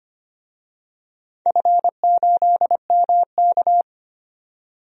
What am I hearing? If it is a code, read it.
Morse code, heard as a keyed tone, about 25 wpm, 700 Hz. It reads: F8MK
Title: DXCC-100 Call Signs - 4 Characters - 25wpm